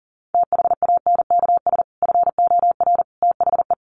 Morse code, heard as a keyed tone, about 40 wpm, 700 Hz. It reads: THANKS FOR THE
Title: Sentences from Top 1500 Words 40wpm